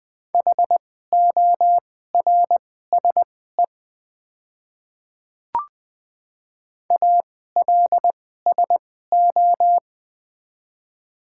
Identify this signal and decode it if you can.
Morse code, heard as a keyed tone, about 20 wpm, spaced as for 14 wpm, 700 Hz. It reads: HORSE ALSO
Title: 2x Character Spacing - Top 500 Words - 20wpm